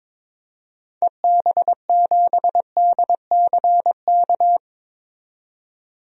Morse code, heard as a keyed tone, about 22 wpm, 700 Hz. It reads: EB7DCK
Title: DXCC-100 Call Signs - 6 Characters - 22wpm